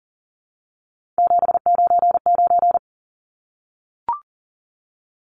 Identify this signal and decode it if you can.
Morse code, heard as a keyed tone, about 40 wpm, 700 Hz. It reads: 799
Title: Numbers - 3-Digit 40wpm